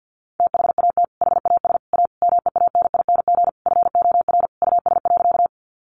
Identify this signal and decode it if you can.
Morse code, heard as a keyed tone, about 50 wpm, 700 Hz. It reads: THAT HAS A MEANING FOR US.
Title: Sentences from Top 900 Words 50wpm